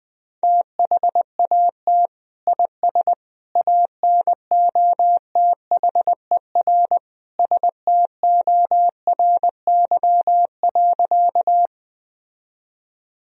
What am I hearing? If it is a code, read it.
Morse code, heard as a keyed tone, about 20 wpm, 700 Hz. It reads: THAT IS ANOTHER STORY.
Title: Sentences from Top 1000 Words - Review Part 1 - 20wpm